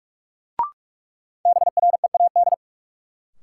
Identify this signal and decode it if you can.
Morse code, heard as a keyed tone, about 46 wpm, 700 Hz. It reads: BREAD
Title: Sentences from Top 1600 Words - New Word Drill Alt - 45wpm